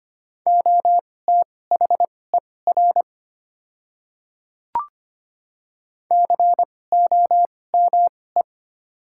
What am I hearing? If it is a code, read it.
Morse code, heard as a keyed tone, about 25 wpm, spaced as for 18 wpm, 700 Hz. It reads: OTHER COME
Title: 2x Character Spacing - Top 100 Words - 25wpm